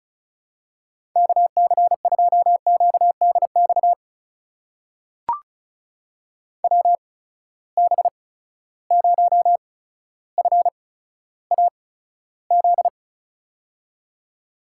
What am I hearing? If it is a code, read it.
Morse code, heard as a keyed tone, about 36 wpm, spaced as for 15 wpm, 700 Hz. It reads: KC2QDX WB0FAZ
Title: USA Call Signs - 6 Characters - 8x Character Spacing - 35wpm